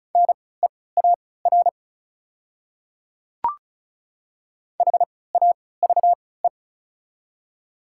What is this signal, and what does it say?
Morse code, heard as a keyed tone, about 34 wpm, spaced as for 20 wpm, 700 Hz. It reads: NEAR HAVE
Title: 3x Character Spacing - Top 200 Words - 35wpm